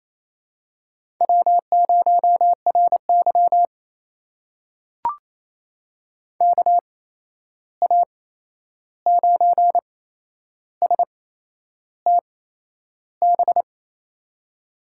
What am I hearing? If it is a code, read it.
Morse code, heard as a keyed tone, about 28 wpm, spaced as for 11 wpm, 700 Hz. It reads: W0RY KA9STB
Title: USA Call Signs - 8x Character Spacing - 28wpm